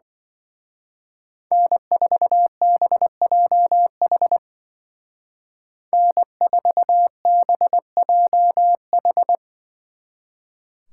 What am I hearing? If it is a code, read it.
Morse code, heard as a keyed tone, about 24 wpm, 700 Hz. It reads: N4BJH N4BJH
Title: Speed Racing - USA 1x3 Call Signs 20wpm